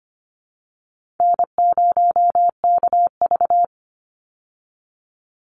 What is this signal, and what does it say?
Morse code, heard as a keyed tone, about 25 wpm, 700 Hz. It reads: N0KV
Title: Call Signs Repeated 3x - 25wpm